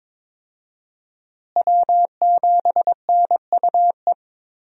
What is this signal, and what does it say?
Morse code, heard as a keyed tone, about 22 wpm, 700 Hz. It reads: W7NUE
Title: USA Call Signs - 5 Characters - Repeated 2x - 22wpm